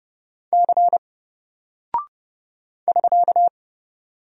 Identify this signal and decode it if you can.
Morse code, heard as a keyed tone, about 30 wpm, 700 Hz. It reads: C <SK>
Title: Single Character (Letter:Number:Punctuation:Prosign) 30wpm